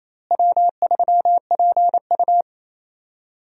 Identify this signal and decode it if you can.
Morse code, heard as a keyed tone, about 28 wpm, 700 Hz. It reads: W3PU
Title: USA Call Signs - 4 Characters - Repeated 2x - 28wpm